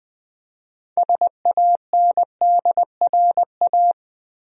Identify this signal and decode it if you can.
Morse code, heard as a keyed tone, about 20 wpm, 700 Hz. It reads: SANDRA
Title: USA Common Names - 20wpm